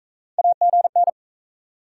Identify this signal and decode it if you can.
Morse code, heard as a keyed tone, about 44 wpm, 700 Hz. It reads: AGN